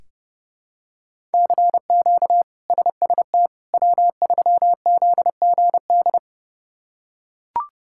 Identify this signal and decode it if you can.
Morse code, heard as a keyed tone, about 30 wpm, 700 Hz. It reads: CQ SST W3ZGD
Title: K1USN SST QSO Practice - 30wpm